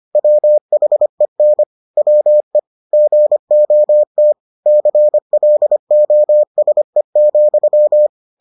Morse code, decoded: WHEN WE GOT CLOSE,

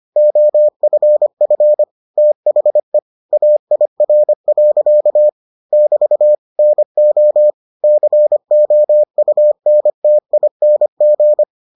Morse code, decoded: OFF THE AIR. <BT> NO COUNTING

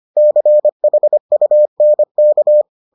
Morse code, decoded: CHUNK

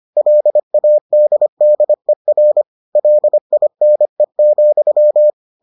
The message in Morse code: LADDER LINE,

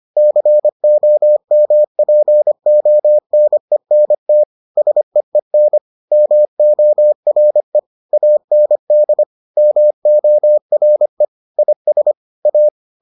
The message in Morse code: COMPONENT SEEN MORE AND MORE IS A

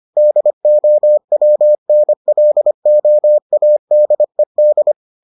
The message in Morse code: DOWNLOADED

